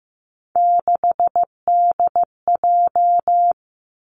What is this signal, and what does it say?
Morse code, heard as a keyed tone, about 15 wpm, 700 Hz. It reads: 6DJ
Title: Three Challenging Letter-Number Combinations 15wpm